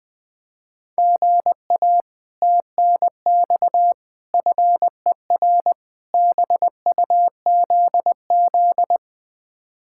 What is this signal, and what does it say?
Morse code, heard as a keyed tone, about 20 wpm, 700 Hz. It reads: GA TNX FER BUZZ
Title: Speed-Racing Common QSO Phrases - 17wpm